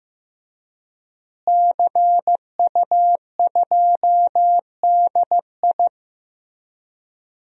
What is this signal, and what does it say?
Morse code, heard as a keyed tone, about 15 wpm, 700 Hz. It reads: CU2DI
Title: DXCC-100 Call Signs - 5 Characters - Repeated 2x - 15wpm